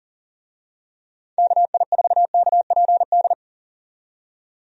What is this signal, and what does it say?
Morse code, heard as a keyed tone, about 40 wpm, 700 Hz. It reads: KI4KPD
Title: USA Call Signs - 6 Characters - 40wpm